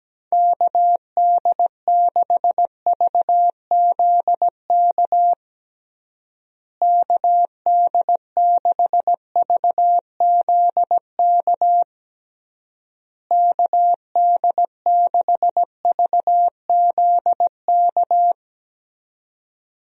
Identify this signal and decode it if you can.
Morse code, heard as a keyed tone, about 17 wpm, 700 Hz. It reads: KD6VZK KD6VZK KD6VZK
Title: USA Call Signs - 6 Characters - Repeated 3x - 17wpm